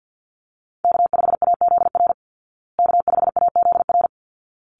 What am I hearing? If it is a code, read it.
Morse code, heard as a keyed tone, about 50 wpm, 700 Hz. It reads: K5AZR K5AZR
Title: Call Signs Repeated 3x - 50wpm